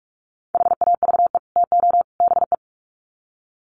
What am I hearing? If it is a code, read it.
Morse code, heard as a keyed tone, about 44 wpm, 700 Hz. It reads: HAVE TO BE
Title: Sets of 3 Words 45wpm